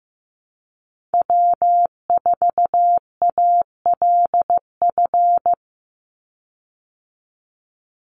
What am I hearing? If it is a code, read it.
Morse code, heard as a keyed tone, about 15 wpm, 700 Hz. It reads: W4ALF
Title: Call Signs Repeated 2x - 15wpm